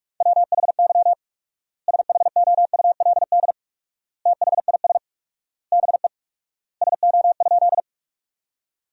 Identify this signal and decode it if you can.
Morse code, heard as a keyed tone, about 47 wpm, 700 Hz. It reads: WHY SHOULD THIS BE SO?
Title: 4x Word-Spacing - Sentences from Top 300 Words - 45wpm